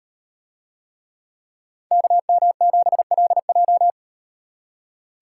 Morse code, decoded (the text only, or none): KM7LJ